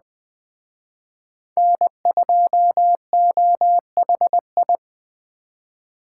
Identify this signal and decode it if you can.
Morse code, heard as a keyed tone, about 20 wpm, 700 Hz. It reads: N2OHI